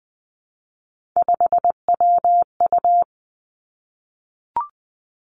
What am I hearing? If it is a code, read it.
Morse code, heard as a keyed tone, about 20 wpm, 700 Hz. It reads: 5WU
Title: Three Challenging Letter-Number Combinations 20wpm